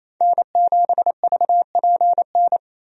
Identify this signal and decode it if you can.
Morse code, heard as a keyed tone, about 28 wpm, 700 Hz. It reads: N7VPN
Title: USA Call Signs - 5 Characters - 28wpm